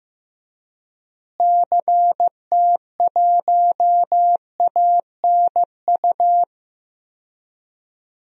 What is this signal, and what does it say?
Morse code, heard as a keyed tone, about 16 wpm, 700 Hz. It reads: CT1ANU
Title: DXCC-100 Call Signs - 6 Characters - Repeated 3x - 15wpm